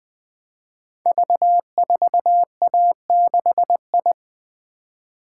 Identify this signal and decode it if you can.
Morse code, heard as a keyed tone, about 21 wpm, 700 Hz. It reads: V4A6I
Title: Code Groups - 5 Letter:Numbers - 20wpm